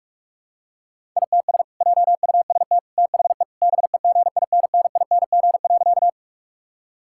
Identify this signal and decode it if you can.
Morse code, heard as a keyed tone, about 41 wpm, 700 Hz. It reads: ITS JUST THE BEGINNING.
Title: Sentences from Top 1000 Words - Review Part 5 - 45wpm